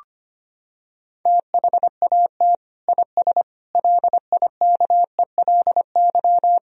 Morse code, heard as a keyed tone, about 25 wpm, 700 Hz. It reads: THAT IS LIKELY
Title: Sentences from Top 1000 Words - Review Part 4 - 25wpm